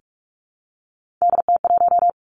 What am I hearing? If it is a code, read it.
Morse code, heard as a keyed tone, about 46 wpm, 700 Hz. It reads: DT1
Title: Three Challenging Letter-Number Combinations 45wpm